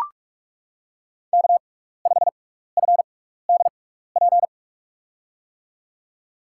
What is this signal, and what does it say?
Morse code, heard as a keyed tone, about 44 wpm, spaced as for 15 wpm, 700 Hz. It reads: K5FDP
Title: USA Call Signs - 6x Character Spacing - 45wpm